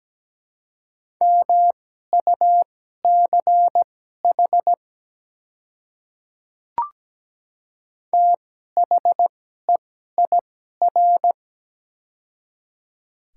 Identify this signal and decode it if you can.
Morse code, heard as a keyed tone, about 17 wpm, spaced as for 12 wpm, 700 Hz. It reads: MUCH THEIR
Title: 2x Character Spacing - Top 200 Words - 17wpm